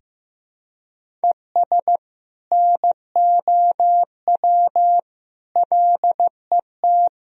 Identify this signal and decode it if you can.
Morse code, heard as a keyed tone, about 15 wpm, 700 Hz. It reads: ES NOW LET
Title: Sentences from Top 300 Words and Abbreviations 15wpm